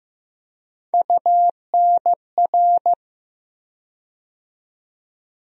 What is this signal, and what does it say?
Morse code, heard as a keyed tone, about 15 wpm, 700 Hz. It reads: UNR
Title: USA Call Sign Postfixes - 3 Letters 15wpm